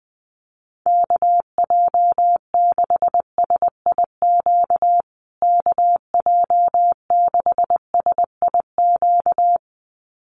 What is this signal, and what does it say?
Morse code, heard as a keyed tone, about 20 wpm, 700 Hz. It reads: KJ6SIQ KJ6SIQ
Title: USA Call Signs Repeated 2x 20wpm